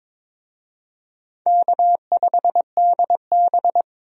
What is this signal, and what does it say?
Morse code, heard as a keyed tone, about 23 wpm, 700 Hz. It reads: K5DB
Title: USA Call Signs - 4 Characters - 22wpm